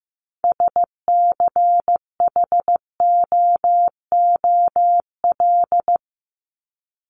Morse code, heard as a keyed tone, about 15 wpm, 700 Hz. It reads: SCHOOL
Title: Top 300 Words 15wpm